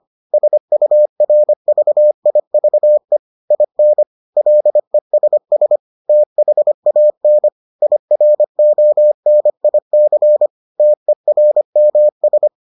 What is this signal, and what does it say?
Morse code, heard as a keyed tone, about 25 wpm, 600 Hz. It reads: SURVIVE IN LESS THAN IRONIC TERMS